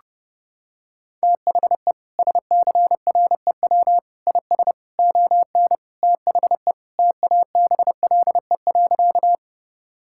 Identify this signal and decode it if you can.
Morse code, heard as a keyed tone, about 32 wpm, 700 Hz. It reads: THE SCREW IS ON THE TABLE.
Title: Sentences from Top 1600 Words 30wpm